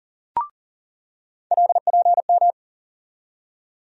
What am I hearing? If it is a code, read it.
Morse code, heard as a keyed tone, about 40 wpm, 700 Hz. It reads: LPM
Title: USA Call Sign Postfixes - 3 Letters 40wpm